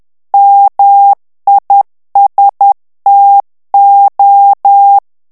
Morse code, heard as a keyed tone, about 10 wpm, 800 Hz. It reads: MISTO
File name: ITALIEN texte clair 2 vitesse 10 mots minute